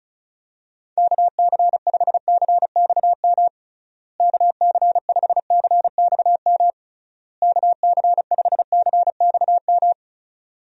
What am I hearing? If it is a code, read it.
Morse code, heard as a keyed tone, about 36 wpm, 700 Hz. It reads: KC5CXM KC5CXM KC5CXM
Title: USA Call Signs - 6 Characters - Repeated 3x - 35wpm